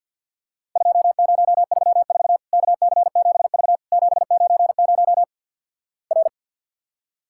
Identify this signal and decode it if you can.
Morse code, heard as a keyed tone, about 50 wpm, 700 Hz. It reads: 2034 KC7V 790 R
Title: Worked All Europe DX Contest - QTC Exchange - 50wpm